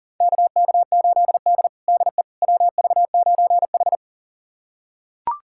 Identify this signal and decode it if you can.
Morse code, heard as a keyed tone, about 40 wpm, 700 Hz. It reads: KK8D DE WV9H
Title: USA Callsign DE Callsign - 4 Characters - 40wpm